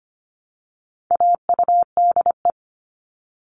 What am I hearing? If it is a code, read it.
Morse code, heard as a keyed tone, about 25 wpm, 700 Hz. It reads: AUDE
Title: All Cut Numbers - 4-Digit 25wpm